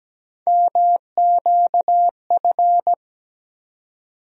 Morse code, heard as a keyed tone, about 17 wpm, 700 Hz. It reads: MQF
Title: USA Call Sign Postfixes - 3 Letters 17wpm